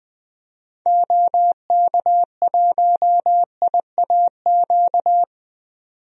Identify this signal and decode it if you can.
Morse code, heard as a keyed tone, about 20 wpm, 700 Hz. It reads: OK1IAQ
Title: DXCC-100 Call Signs - 6 Characters - 20wpm